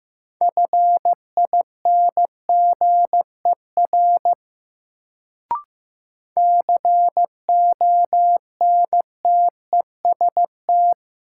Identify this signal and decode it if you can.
Morse code, heard as a keyed tone, about 15 wpm, 700 Hz. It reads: FINGER CONTEST
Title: Sentences from Top 1600 Words - New Word Drill - 15wpm